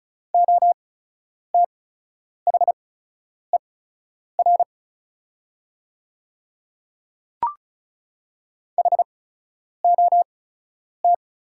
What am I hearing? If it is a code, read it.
Morse code, heard as a keyed tone, about 34 wpm, spaced as for 10 wpm, 700 Hz. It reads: OTHER HOT
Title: 8x Character Spacing - Top 100 Words - 35wpm